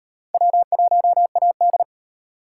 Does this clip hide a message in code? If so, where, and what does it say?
Yes: Morse code from 0.3 to 1.8 s, heard as a keyed tone, about 34 wpm, 700 Hz. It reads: W1AD